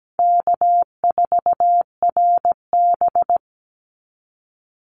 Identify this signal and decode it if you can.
Morse code, heard as a keyed tone, about 17 wpm, 700 Hz. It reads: K4RB
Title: Easy Call Signs Repeated 2x - 17wpm